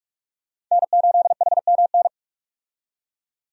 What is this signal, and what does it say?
Morse code, heard as a keyed tone, about 46 wpm, 700 Hz. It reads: N8HMN
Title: USA Call Signs - 5 Characters - 45wpm